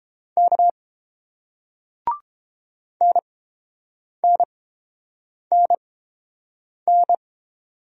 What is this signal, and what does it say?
Morse code, heard as a keyed tone, about 31 wpm, spaced as for 7 wpm, 700 Hz. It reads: K NNNN